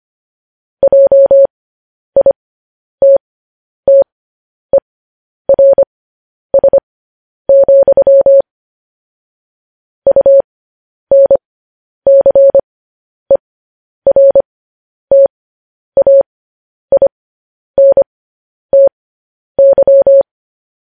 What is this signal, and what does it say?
Morse code, heard as a keyed tone, about 25 wpm, spaced as for 10 wpm, 550 Hz. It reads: JITTERS, UNCERTAINTY
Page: news headlines in morse code at 10 wpm